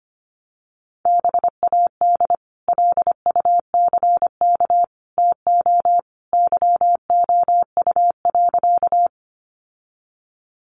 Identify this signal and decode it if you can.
Morse code, heard as a keyed tone, about 25 wpm, 700 Hz. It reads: BAD LUCK TO YOU.
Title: Sentences from Top 1500 Words 25wpm